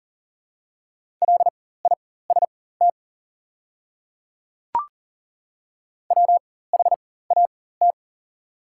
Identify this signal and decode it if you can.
Morse code, heard as a keyed tone, about 38 wpm, spaced as for 18 wpm, 690 Hz. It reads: LIST WHAT
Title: 4x Character Spacing - Top 500 Words - 40wpm